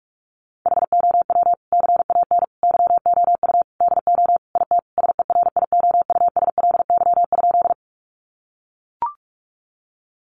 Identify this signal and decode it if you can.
Morse code, heard as a keyed tone, about 44 wpm, 700 Hz. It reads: HOW CAN YOU DO IT SERIOUSLY?
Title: Sentences from Top 1200 Words 45wpm